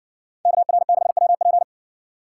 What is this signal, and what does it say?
Morse code, heard as a keyed tone, about 66 wpm, 700 Hz. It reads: KR6WP